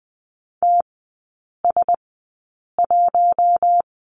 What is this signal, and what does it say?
Morse code, heard as a keyed tone, about 20 wpm, spaced as for 8 wpm, 700 Hz. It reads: TS1
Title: 053 Lesson 10e - Letter r - Three Characters 20wpm